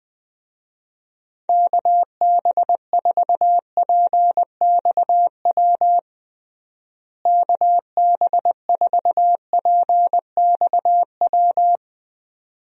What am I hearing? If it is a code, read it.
Morse code, heard as a keyed tone, about 20 wpm, 700 Hz. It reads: KB4PXW KB4PXW
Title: USA Call Signs - 6 Characters - Repeated 2x - 20wpm